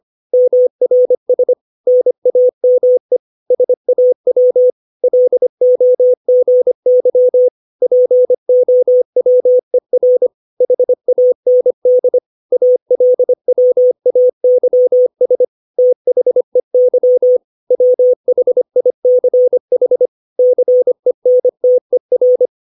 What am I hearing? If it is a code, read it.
Morse code, heard as a keyed tone, about 25 wpm, 500 Hz. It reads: MRS NAME SAW LOGY POWER HAND ALWAYS THEY WHICH CENTER
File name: 632 words / 632 Words 25WPM